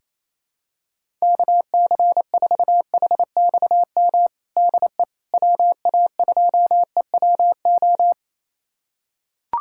Morse code, heard as a keyed tone, about 28 wpm, 700 Hz. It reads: KC4HXM DE WA2EWO
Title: USA Callsign DE Callsign 28wpm